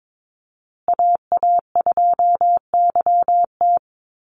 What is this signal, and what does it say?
Morse code, heard as a keyed tone, about 22 wpm, 700 Hz. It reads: AA2YT
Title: USA Call Signs Repeated 3x 22wpm